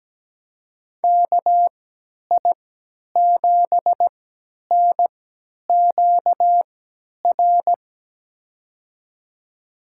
Morse code, heard as a keyed tone, about 17 wpm, spaced as for 10 wpm, 700 Hz. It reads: KI7NQR
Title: USA Call Signs - 6 Characters - 3x Character Spacing - 17wpm